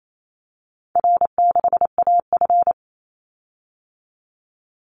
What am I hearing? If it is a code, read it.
Morse code, heard as a keyed tone, about 28 wpm, 700 Hz. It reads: R6AF
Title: Easy Call Signs Repeated 2x - 28wpm